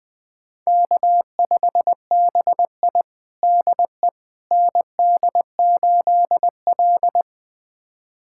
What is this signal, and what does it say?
Morse code, heard as a keyed tone, about 20 wpm, 700 Hz. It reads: K5BI DE ND8L
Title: USA Callsign DE Callsign - 4 Characters - 20wpm